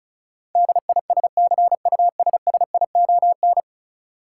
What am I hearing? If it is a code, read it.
Morse code, heard as a keyed tone, about 36 wpm, 700 Hz. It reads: DISCUSSION
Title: Sentences from Top 1100 Words - New Word Drill Alt - 35wpm